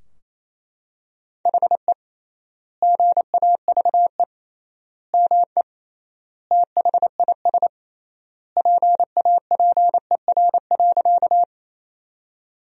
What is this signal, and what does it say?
Morse code, heard as a keyed tone, about 28 wpm, 700 Hz. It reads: HE GAVE ME THIS PAPER.
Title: 3x Word-Spacing - Sentences from Top 500 Words - 28wpm